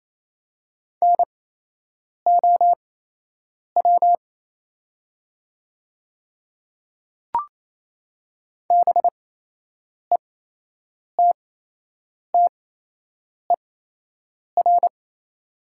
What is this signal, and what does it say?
Morse code, heard as a keyed tone, about 28 wpm, spaced as for 8 wpm, 700 Hz. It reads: NOW BETTER
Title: 8x Character Spacing - Top 500 Words - 28wpm